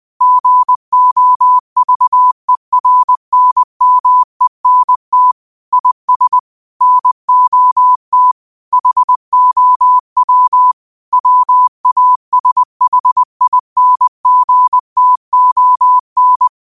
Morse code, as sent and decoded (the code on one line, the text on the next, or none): --. --- ...- . .-. -. -- . -. - / .. ... / -. --- - / .... --- .-- / .-- .- ... .... .. -. --. - --- -.
GOVERNMENT IS NOT HOW WASHINGTON